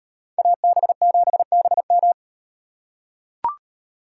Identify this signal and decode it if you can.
Morse code, heard as a keyed tone, about 38 wpm, 700 Hz. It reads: AB7BM